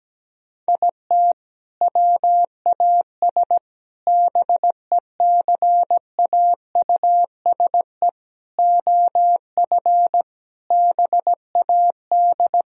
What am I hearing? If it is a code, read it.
Morse code, heard as a keyed tone, about 17 wpm, 700 Hz. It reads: IT WAS BECAUSE OF BAD
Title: Sentences from Top 1000 Words - Review Part 4 - 17wpm